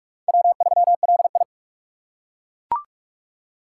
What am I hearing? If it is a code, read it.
Morse code, heard as a keyed tone, about 41 wpm, 700 Hz. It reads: W3LI
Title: USA Call Signs - 6x Character Spacing - 45wpm